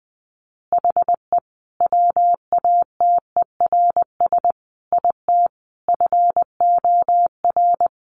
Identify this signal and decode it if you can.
Morse code, heard as a keyed tone, about 20 wpm, 700 Hz. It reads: HE WATERS IT FOR
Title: Easy Phrases 20wpm